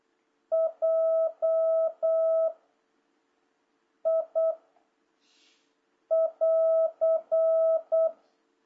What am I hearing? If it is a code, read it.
Morse code, heard as a keyed tone, about 8 wpm, 660 Hz. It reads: J I <AR>
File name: jh8cbh / 2026cw